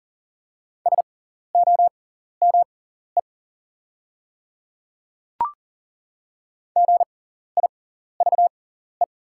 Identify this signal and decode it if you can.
Morse code, heard as a keyed tone, about 40 wpm, spaced as for 14 wpm, 690 Hz. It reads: SOME GIVE
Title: 6x Character Spacing - Top 200 Words - 40wpm